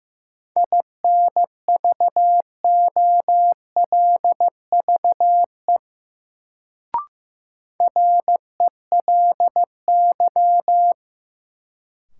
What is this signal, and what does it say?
Morse code, heard as a keyed tone, about 15 wpm, 700 Hz. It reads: INVOLVE RELY
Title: Sentences from Top 1200 Words - New Word Drill - 15wpm